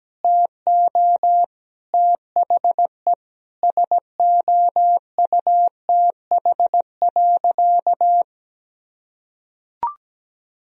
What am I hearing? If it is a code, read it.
Morse code, heard as a keyed tone, about 17 wpm, 700 Hz. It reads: TO THE SOUTH.
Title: Sentences from Top 400 Words and Abbreviations 17wpm